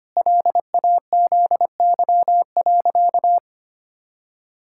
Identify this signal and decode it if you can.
Morse code, heard as a keyed tone, about 25 wpm, 700 Hz. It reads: LAZY.
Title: Pangram Challenge v2 - 25wpm